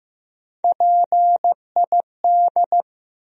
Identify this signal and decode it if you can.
Morse code, heard as a keyed tone, about 16 wpm, 700 Hz. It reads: PID
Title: USA Call Sign Postfixes - 3 Letters 15wpm